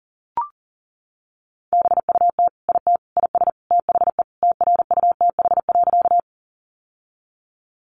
Morse code, heard as a keyed tone, about 40 wpm, 690 Hz. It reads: BUT IT IS THE TRUTH.
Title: Sentences from Top 800 Words 40wpm